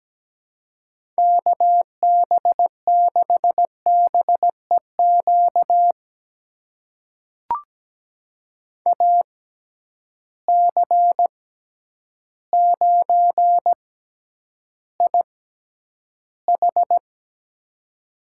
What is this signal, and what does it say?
Morse code, heard as a keyed tone, about 17 wpm, spaced as for 9 wpm, 700 Hz. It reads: KB6BEQ AC9IH
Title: USA Call Signs - 6x Character Spacing - 17wpm